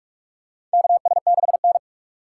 Morse code, heard as a keyed tone, about 46 wpm, 700 Hz. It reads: KS6N
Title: USA Call Signs - 4 Characters - 6x Character Spacing - 45wpm